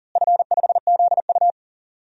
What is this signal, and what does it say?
Morse code, heard as a keyed tone, about 40 wpm, 700 Hz. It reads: F5ZU
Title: DXCC-100 Call Signs - 4 Characters - Repeated 3x - 40wpm